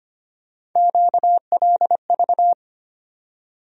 Morse code, heard as a keyed tone, about 25 wpm, 700 Hz. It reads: QLV
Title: USA Call Sign Postfixes - 3 Letters 25wpm